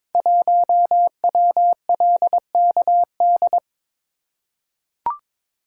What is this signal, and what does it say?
Morse code, heard as a keyed tone, about 22 wpm, 700 Hz. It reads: 1WLKD
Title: Code Groups - 5 Letter:Numbers - 22wpm